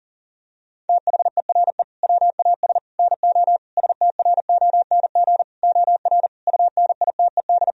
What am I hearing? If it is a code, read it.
Morse code, heard as a keyed tone, about 40 wpm, 700 Hz. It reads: THERE WAS NO STRONG OR UNITED
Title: Sentences from Top 1000 Words - Review Part 4 - 40wpm